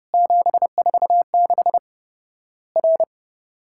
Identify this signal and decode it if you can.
Morse code, heard as a keyed tone, about 30 wpm, 700 Hz. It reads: 746 R
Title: Worked All Europe DX Contest - QTC Exchange - 30wpm